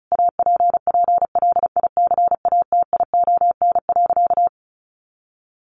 Morse code, heard as a keyed tone, about 34 wpm, 700 Hz. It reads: APPLICATION.
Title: Sentences from Top 1200 Words 35wpm